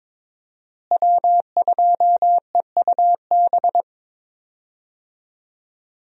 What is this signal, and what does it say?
Morse code, heard as a keyed tone, about 22 wpm, 700 Hz. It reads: W2EUB
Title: USA Call Signs - 5 Characters - Repeated 2x - 22wpm